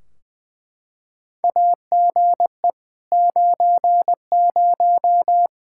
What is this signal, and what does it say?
Morse code, heard as a keyed tone, about 20 wpm, 700 Hz. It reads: AGE 90
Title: Practice QSO - Exchange 3 - More Info - 20wpm